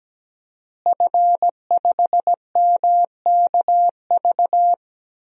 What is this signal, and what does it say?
Morse code, heard as a keyed tone, about 17 wpm, 700 Hz. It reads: F5MKV
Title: DXCC-100 Call Signs - 5 Characters - Repeated 3x - 17wpm